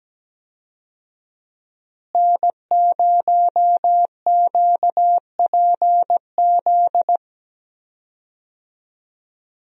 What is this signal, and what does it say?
Morse code, heard as a keyed tone, about 17 wpm, 700 Hz. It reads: N0QPZ
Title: USA Call Signs - 5 Characters - Repeated 3x - 17wpm